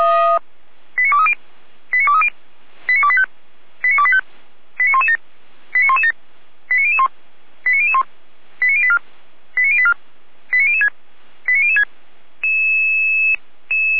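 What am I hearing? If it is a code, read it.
Five-tone selective call (ZVEI), heard as quick runs of five tones, tones of about 70 ms. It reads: 89230 89230 89285 89285 89117 89117 89001 89001 89084 89084 89006 89006
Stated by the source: Five-tone ZVEI selective call received on an SDR.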